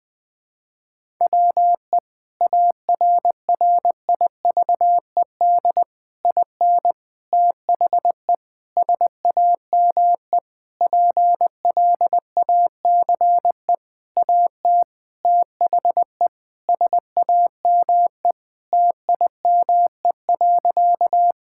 Morse code, heard as a keyed tone, about 20 wpm, 700 Hz. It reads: WE ARRIVED IN THE SAME PLACE AT THE SAME TIME.
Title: Sentences with 8-Word Phrases - Spoken Beginning - 20wpm